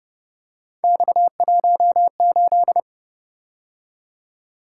Morse code, heard as a keyed tone, about 26 wpm, 700 Hz. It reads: X18
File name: Speed-Racing - Three Letter-Number Combinations - 22wpm